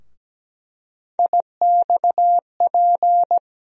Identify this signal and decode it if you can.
Morse code, heard as a keyed tone, about 17 wpm, 700 Hz. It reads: IXP
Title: USA Call Sign Postfixes - 3 Letters 17wpm